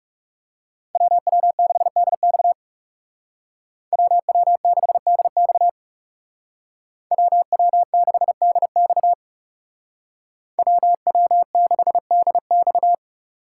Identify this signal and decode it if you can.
Morse code, heard as a keyed tone, about 37 wpm, 700 Hz. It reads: WW6DX WW6DX WW6DX WW6DX
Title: Speed Racing - USA 2x2 Call Signs 30wpm